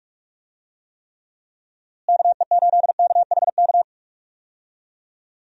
Morse code, heard as a keyed tone, about 46 wpm, 700 Hz. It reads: KE8KHK
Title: USA Call Signs - 6 Characters - Repeated 3x - 45wpm